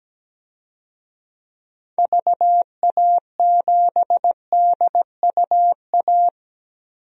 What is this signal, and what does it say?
Morse code, heard as a keyed tone, about 17 wpm, 700 Hz. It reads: VA7DUA
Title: DXCC-100 Call Signs - 6 Characters - Repeated 2x - 17wpm